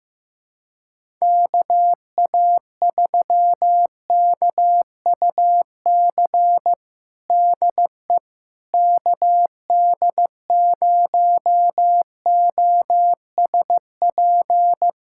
Morse code, heard as a keyed tone, about 15 wpm, 700 Hz. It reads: KA3KUC DE KD0OSP
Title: USA Callsign DE Callsign 15wpm